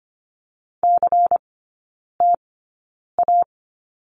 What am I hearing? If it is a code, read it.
Morse code, heard as a keyed tone, about 25 wpm, spaced as for 9 wpm, 700 Hz. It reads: CTA
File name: Lesson 17e - Letter u - Three Characters 25wpm